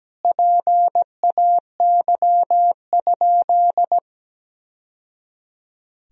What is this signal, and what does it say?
Morse code, heard as a keyed tone, about 17 wpm, 700 Hz. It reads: PAY?
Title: Sentences from Top 1000 Words - Review Part 3 - 17wpm